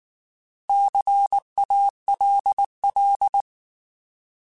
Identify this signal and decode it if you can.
Morse code, heard as a keyed tone, about 19 wpm, 800 Hz. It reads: CALL
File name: Alphabetical / Alphabetical mp3s / Cs @18